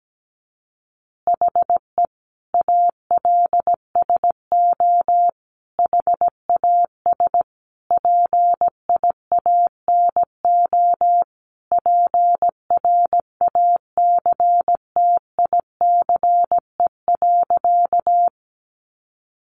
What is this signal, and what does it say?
Morse code, heard as a keyed tone, about 17 wpm, 700 Hz. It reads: HE ALSO HAS PIANO PRACTICE.
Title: Additional Easy Phrases 17wpm